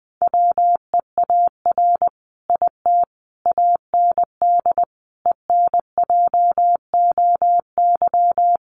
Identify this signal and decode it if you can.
Morse code, heard as a keyed tone, about 20 wpm, 700 Hz. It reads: WEAR IT AND ENJOY
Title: Sentences from Top 1500 Words 20wpm